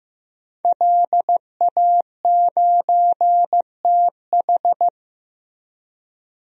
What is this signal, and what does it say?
Morse code, heard as a keyed tone, about 15 wpm, 700 Hz. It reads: LA9TH
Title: DXCC-100 Call Signs - 5 Characters - Repeated 3x - 15wpm